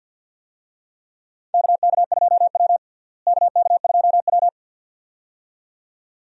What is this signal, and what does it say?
Morse code, heard as a keyed tone, about 47 wpm, 700 Hz. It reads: KK2W KK2W
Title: USA Call Signs - 4 Characters - Repeated 2x - 50wpm